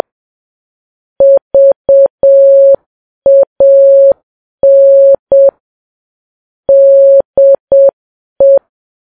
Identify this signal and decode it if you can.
Morse code, heard as a keyed tone, about 7 wpm, 550 Hz. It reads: VAN DE
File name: Files Web tls / Morse on9fv / morse7kps